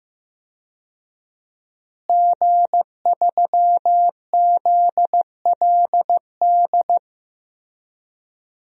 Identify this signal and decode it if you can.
Morse code, heard as a keyed tone, about 15 wpm, 700 Hz. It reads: G3ZLD
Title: DXCC-100 Call Signs - 5 Characters - Repeated 3x - 15wpm